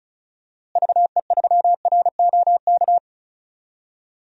Morse code, heard as a keyed tone, about 34 wpm, 700 Hz. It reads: VE3ROK
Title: DXCC-100 Call Signs - 6 Characters - Repeated 2x - 35wpm